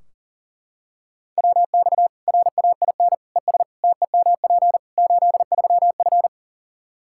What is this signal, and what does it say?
Morse code, heard as a keyed tone, about 40 wpm, 700 Hz. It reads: WX RAIN ES TEMP 83F
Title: Practice QSO - Exchange 2 - Info - 40wpm